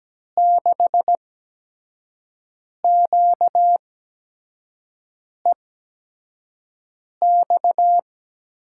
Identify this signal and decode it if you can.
Morse code, heard as a keyed tone, about 17 wpm, spaced as for 5 wpm, 700 Hz. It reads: 6QEX
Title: USA Call Signs - 6 Characters - 8x Character Spacing - 17wpm